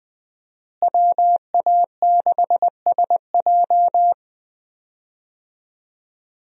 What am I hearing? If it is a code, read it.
Morse code, heard as a keyed tone, about 20 wpm, 700 Hz. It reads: WA6SJ
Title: USA Call Signs - 5 Characters - Repeated 2x - 20wpm